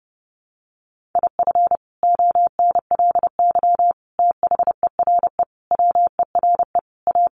Morse code, heard as a keyed tone, about 30 wpm, 700 Hz. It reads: IF ONLY THERE WERE A
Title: Sentences from Top 1400 Words 30wpm